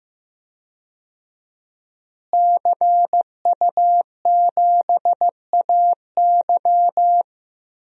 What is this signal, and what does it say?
Morse code, heard as a keyed tone, about 15 wpm, 700 Hz. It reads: CU7AY
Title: DXCC-100 Call Signs - 5 Characters - Repeated 3x - 15wpm